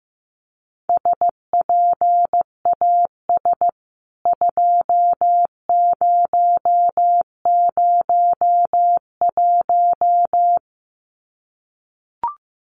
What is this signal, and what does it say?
Morse code, heard as a keyed tone, about 15 wpm, 700 Hz. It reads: SPAS 2001
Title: CWT Exchange 15wpm